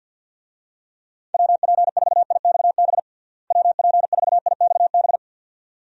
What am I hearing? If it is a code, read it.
Morse code, heard as a keyed tone, about 47 wpm, 700 Hz. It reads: WP4IXB WP4IXB
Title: USA Call Signs - 6 Characters - Repeated 2x - 50wpm